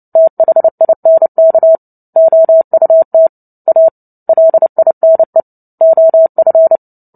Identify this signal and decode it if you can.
Morse code, heard as a keyed tone, about 30 wpm, 650 Hz. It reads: THINK OUT A LINE OF
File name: Book Two / 03-Chapter 3 - 29 WPM